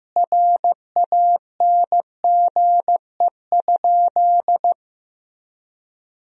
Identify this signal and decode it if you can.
Morse code, heard as a keyed tone, about 16 wpm, 700 Hz. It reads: RANGE?
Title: Sentences from Top 1000 Words - Review Part 4 - 15wpm